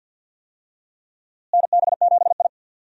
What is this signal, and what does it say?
Morse code, heard as a keyed tone, about 52 wpm, 700 Hz. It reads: NB7I
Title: USA Call Signs - 4 Characters - Repeated 3x - 50wpm